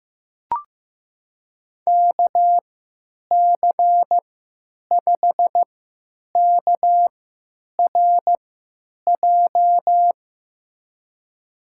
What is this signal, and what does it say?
Morse code, heard as a keyed tone, about 15 wpm, spaced as for 9 wpm, 700 Hz. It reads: KC5KRJ
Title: USA Call Signs - 6 Characters - 3x Character Spacing - 15wpm